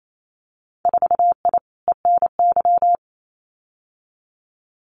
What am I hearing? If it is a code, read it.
Morse code, heard as a keyed tone, about 28 wpm, 700 Hz. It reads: 4I ENY
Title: Winter Field Day Exchanges 28wpm